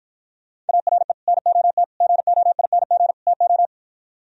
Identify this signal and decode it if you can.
Morse code, heard as a keyed tone, about 54 wpm, 700 Hz. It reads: ARE NOT GOING TO